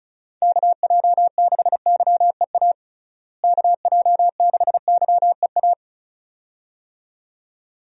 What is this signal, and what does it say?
Morse code, heard as a keyed tone, about 36 wpm, 700 Hz. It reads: KJ6YEA KJ6YEA
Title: USA Call Signs - 6 Characters - Repeated 2x - 35wpm